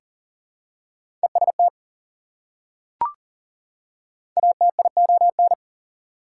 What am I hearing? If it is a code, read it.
Morse code, heard as a keyed tone, about 40 wpm, 700 Hz. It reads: EST ATION
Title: Prefixes and Suffixes 40wpm